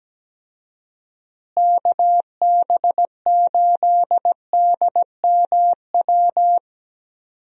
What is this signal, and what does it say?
Morse code, heard as a keyed tone, about 17 wpm, 700 Hz. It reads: KB8DMW
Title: USA Call Signs - 6 Characters - Repeated 2x - 17wpm